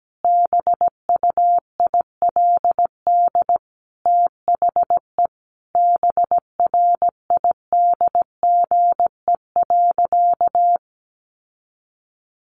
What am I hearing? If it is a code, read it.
Morse code, heard as a keyed tone, about 17 wpm, 700 Hz. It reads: BUILD THE BRIDGE.
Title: Sentences from Top 1500 Words 17wpm